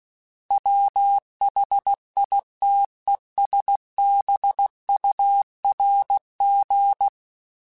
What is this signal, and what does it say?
Morse code, heard as a keyed tone, about 16 wpm, 800 Hz. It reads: WHITESBURG
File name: fromSMs / FromSMs mp3s / Cities@15